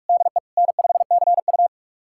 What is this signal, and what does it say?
Morse code, heard as a keyed tone, about 46 wpm, 700 Hz. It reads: DE N5CU
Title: USA Callsign DE Callsign - 4 Characters - 45wpm